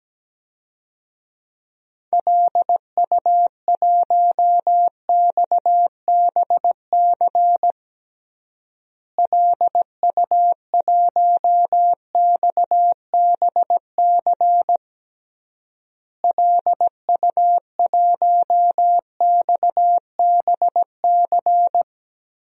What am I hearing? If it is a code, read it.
Morse code, heard as a keyed tone, about 17 wpm, 700 Hz. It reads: LU1XBC LU1XBC LU1XBC
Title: DXCC-100 Call Signs - 6 Characters - Repeated 3x - 17wpm